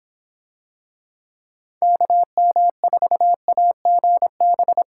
Morse code, heard as a keyed tone, about 25 wpm, 700 Hz. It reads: KM4AGB